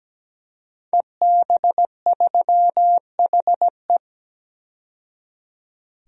Morse code, heard as a keyed tone, about 17 wpm, 700 Hz. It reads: EB3HE
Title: DXCC-100 Call Signs - 5 Characters - 17wpm